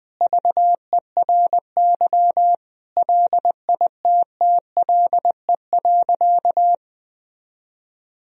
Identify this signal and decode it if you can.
Morse code, heard as a keyed tone, about 20 wpm, 700 Hz. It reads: VERY LITTLE.
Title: Sentences from Top 1000 Words - Review Part 3 - 20wpm